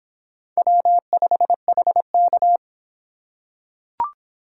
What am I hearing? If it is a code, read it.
Morse code, heard as a keyed tone, about 24 wpm, 700 Hz. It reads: W5HK